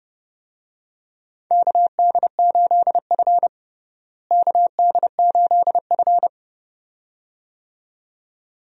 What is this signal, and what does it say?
Morse code, heard as a keyed tone, about 32 wpm, 700 Hz. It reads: KD8F KD8F
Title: USA Call Signs - 4 Characters - Repeated 2x - 30wpm